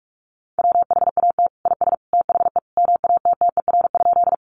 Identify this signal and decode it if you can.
Morse code, heard as a keyed tone, about 44 wpm, 700 Hz. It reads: WHAT IS THE MATTER?
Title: Sentences from Top 700 Words 45wpm